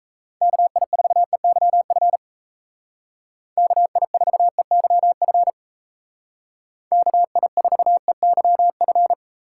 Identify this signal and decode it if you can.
Morse code, heard as a keyed tone, about 33 wpm, 700 Hz. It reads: KI4EYF KI4EYF KI4EYF
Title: Speed Racing - USA 2x3 Call Signs 28wpm